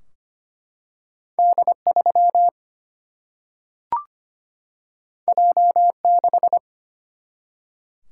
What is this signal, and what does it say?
Morse code, heard as a keyed tone, about 25 wpm, 700 Hz. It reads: D3 J6
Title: Single-Letter Single-Number 25wpm